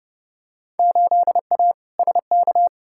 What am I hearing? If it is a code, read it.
Morse code, heard as a keyed tone, about 30 wpm, 700 Hz. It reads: 8A SK
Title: Field Day Exchanges Encore - 30wpm